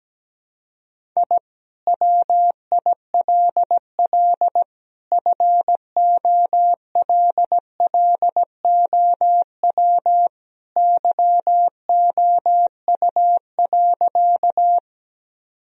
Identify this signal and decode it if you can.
Morse code, heard as a keyed tone, about 17 wpm, 700 Hz. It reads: I WILL FOLLOW YOU.
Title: Sentences from Top 1000 Words - Review Part 1 - 17wpm